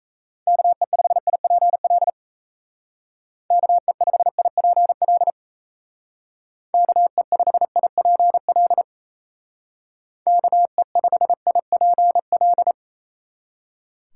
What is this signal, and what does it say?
Morse code, heard as a keyed tone, about 37 wpm, 700 Hz. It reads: KE5IPL KE5IPL KE5IPL KE5IPL